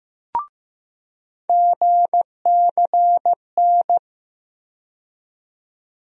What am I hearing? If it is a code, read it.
Morse code, heard as a keyed tone, about 15 wpm, 700 Hz. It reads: GCN
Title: USA Call Sign Postfixes - 3 Letters 15wpm